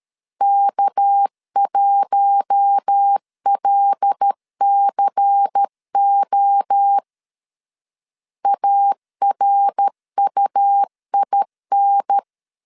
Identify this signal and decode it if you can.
Morse code, heard as a keyed tone, about 13 wpm, 800 Hz. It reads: K1LCO ARUIN
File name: LEZIONI / 6 PKX0 / 6 PKX 12wpm 2024